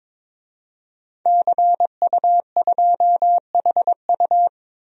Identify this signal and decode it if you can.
Morse code, heard as a keyed tone, about 22 wpm, 700 Hz. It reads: CU2HU
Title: DXCC-100 Call Signs - 5 Characters - 22wpm